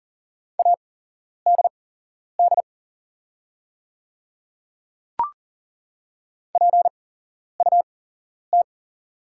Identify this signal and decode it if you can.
Morse code, heard as a keyed tone, about 38 wpm, spaced as for 11 wpm, 700 Hz. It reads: ADD PUT
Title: 8x Character Spacing - Top 300 Words - 40wpm